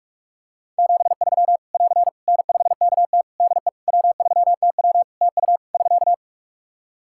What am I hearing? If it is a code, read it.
Morse code, heard as a keyed tone, about 43 wpm, 700 Hz. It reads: 73 <AR> N5KT DE W3TW TU <SK>
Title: Practice QSO - Exchange 4 - Ending - 45wpm